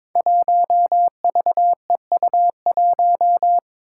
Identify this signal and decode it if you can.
Morse code, heard as a keyed tone, about 21 wpm, 700 Hz. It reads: 1VEU1
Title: Five Challenging Letter-Number Combinations 22wpm